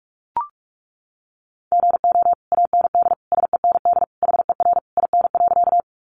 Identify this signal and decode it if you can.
Morse code, heard as a keyed tone, about 44 wpm, 700 Hz. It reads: GO AND SEND HER IN.
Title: Sentences from Top 1500 Words 45wpm